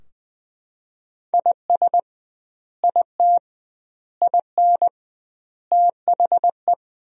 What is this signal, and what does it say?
Morse code, heard as a keyed tone, about 20 wpm, 700 Hz. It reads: IS IT IN THE
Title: 2x Word-Spacing - Sentences from Top 300 Words - 20wpm